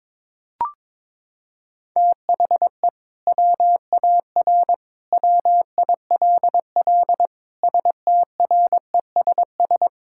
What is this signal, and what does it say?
Morse code, heard as a keyed tone, about 22 wpm, 700 Hz. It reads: THE WAR WILL STRESS
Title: Sentences from Top 1000 Words - Review Part 6 - 22wpm